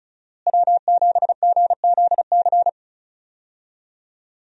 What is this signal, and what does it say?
Morse code, heard as a keyed tone, about 34 wpm, 700 Hz. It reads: W7GZC
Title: USA Call Signs - 5 Characters - Repeated 2x - 35wpm